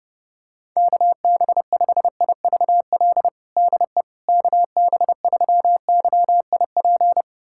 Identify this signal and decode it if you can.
Morse code, heard as a keyed tone, about 30 wpm, 700 Hz. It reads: KB5IVL DE KB3YIP
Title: USA Callsign DE Callsign - 6 Characters - 30wpm